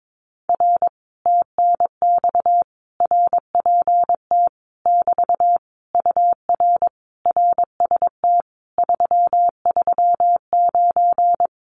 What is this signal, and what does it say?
Morse code, heard as a keyed tone, about 22 wpm, 700 Hz. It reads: R TNX RPT <BT> UR RST 339
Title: Common QSO Phrases Encore 22wpm